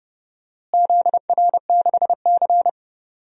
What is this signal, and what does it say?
Morse code, heard as a keyed tone, about 30 wpm, 700 Hz. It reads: ZR6C
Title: DXCC-100 Call Signs - 4 Characters - Repeated 3x - 30wpm